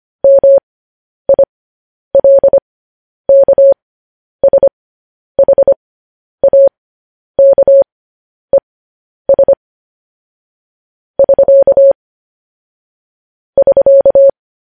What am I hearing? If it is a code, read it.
Morse code, heard as a keyed tone, about 25 wpm, spaced as for 10 wpm, 550 Hz. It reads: MILKSHAKES <SK> <SK>
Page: news headlines in morse code at 10 wpm